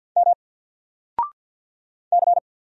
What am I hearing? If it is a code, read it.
Morse code, heard as a keyed tone, about 49 wpm, 700 Hz. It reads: M C
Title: Single Character (Letter:Number:Punctuation:Prosign) Alt 50wpm